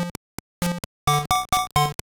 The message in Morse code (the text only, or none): none